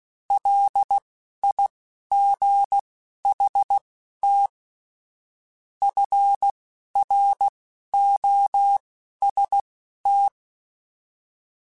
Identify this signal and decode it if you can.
Morse code, heard as a keyed tone, about 16 wpm, spaced as for 11 wpm, 800 Hz. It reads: LIGHT FROST
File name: Weather Sets / Weather Sets-mp3s / Weather 4 24 @10